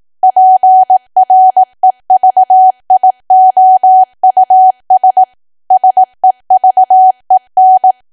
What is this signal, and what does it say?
Morse code, heard as a keyed tone, about 18 wpm, 750 Hz. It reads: PREVIOUS SEVEN
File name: cwa / W1AW-Level2 / 18WPM